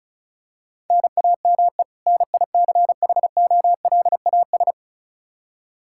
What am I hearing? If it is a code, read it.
Morse code, heard as a keyed tone, about 34 wpm, 700 Hz. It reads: NAME NICHOLAS
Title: K1USN SST QSO Practice 35wpm